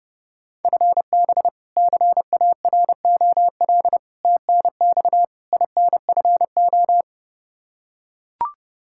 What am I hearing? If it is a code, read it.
Morse code, heard as a keyed tone, about 30 wpm, 700 Hz. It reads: FB CAROL TNX INFO
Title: Practice QSO - Exchange 3 - More Info - 30wpm